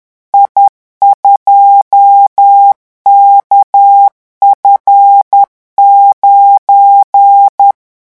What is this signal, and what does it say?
Morse code, heard as a keyed tone, about 10 wpm, 800 Hz. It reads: I2KF9
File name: morse mp3 / revision16-vitesse 10 mots